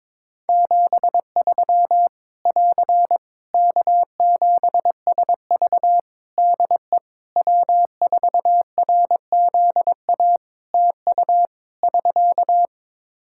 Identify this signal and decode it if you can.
Morse code, heard as a keyed tone, about 22 wpm, 700 Hz. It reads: 73 <AR> K7SV DE W4RZA TU <SK>
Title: Practice QSO - Exchange 4 - Ending - 22wpm